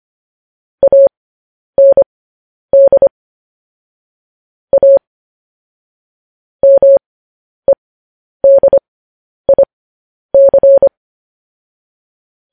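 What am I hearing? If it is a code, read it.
Morse code, heard as a keyed tone, about 25 wpm, spaced as for 10 wpm, 550 Hz. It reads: AND A MEDIC
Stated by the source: Morse code news headlines, likely sent at 10 WPM.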